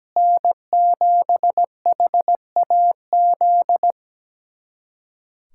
Morse code, heard as a keyed tone, about 17 wpm, 700 Hz. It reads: N7HAZ
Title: USA Callsign DE Callsign - 5 Characters - 17wpm